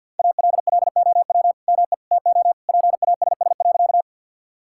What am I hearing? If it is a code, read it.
Morse code, heard as a keyed tone, about 48 wpm, 700 Hz. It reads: ALLOW ME TO PASS.
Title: Sentences from Top 1000 Words - Review Part 5 - 50wpm